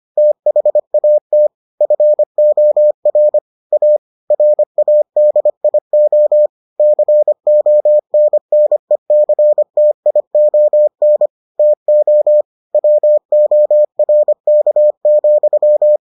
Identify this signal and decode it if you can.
Morse code, heard as a keyed tone, about 25 wpm, 600 Hz. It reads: THAT FOR A RADIO CONNECTION TO WORK,